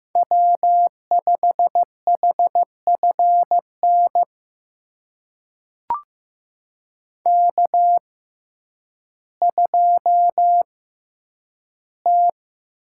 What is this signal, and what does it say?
Morse code, heard as a keyed tone, about 15 wpm, spaced as for 9 wpm, 700 Hz. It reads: W5HFN K2T
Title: USA Call Signs - 5 Characters - 6x Character Spacing - 15wpm